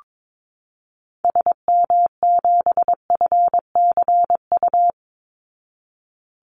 Morse code, heard as a keyed tone, about 22 wpm, 700 Hz. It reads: SM7FCU
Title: Call Signs Repeated 2x - 22wpm